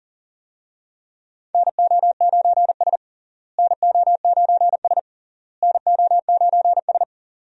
Morse code, heard as a keyed tone, about 40 wpm, 700 Hz. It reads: NO9S NO9S NO9S
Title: USA Call Signs - 4 Characters - Repeated 3x - 40wpm